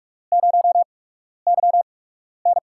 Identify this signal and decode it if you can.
Morse code, heard as a keyed tone, about 44 wpm, 700 Hz. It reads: 0 Y N
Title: USA Call Signs - 4 Characters - 8x Character Spacing - 45wpm